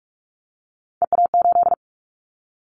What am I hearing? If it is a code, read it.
Morse code, heard as a keyed tone, about 46 wpm, 700 Hz. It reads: EA8
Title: Callsign Prefixes for Top 100 DXCC - 45wpm